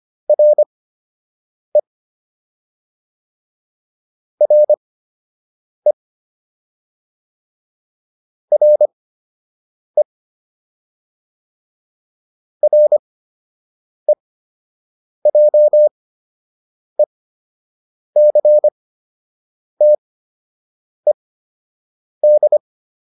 Morse code, heard as a keyed tone, about 25 wpm, spaced as for 7 wpm, 600 Hz. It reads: RE RE RE REJECTED
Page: fundamental-practice-files